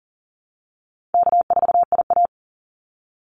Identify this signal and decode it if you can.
Morse code, heard as a keyed tone, about 40 wpm, 700 Hz. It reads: K4IA
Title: Easy Call Signs - 40wpm